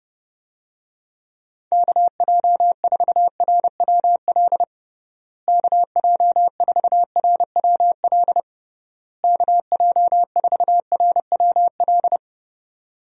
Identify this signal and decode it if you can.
Morse code, heard as a keyed tone, about 30 wpm, 700 Hz. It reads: KJ4RWL KJ4RWL KJ4RWL
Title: USA Call Signs - 6 Characters - Repeated 3x - 30wpm